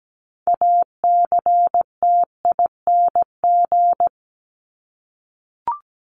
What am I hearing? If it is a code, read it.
Morse code, heard as a keyed tone, about 17 wpm, 700 Hz. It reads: ACTING
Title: ING Suffix - Top 100 Words with Base Words 17wpm